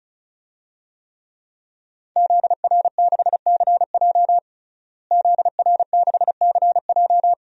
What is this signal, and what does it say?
Morse code, heard as a keyed tone, about 36 wpm, 700 Hz. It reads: ZR6CJ ZR6CJ
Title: DXCC-100 Call Signs - 5 Characters - Repeated 2x - 35wpm